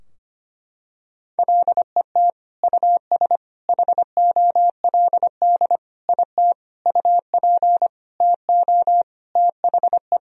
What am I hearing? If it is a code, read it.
Morse code, heard as a keyed tone, about 25 wpm, 700 Hz. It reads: LET US HOLD IT UP TO THE
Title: Sentences from Top 1000 Words - Review Part 2 - 25wpm